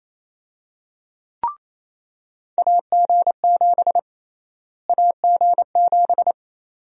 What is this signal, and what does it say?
Morse code, heard as a keyed tone, about 28 wpm, 700 Hz. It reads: AG7 AG7
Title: USA Call Sign Prefixes - Repeated 2x - 28wpm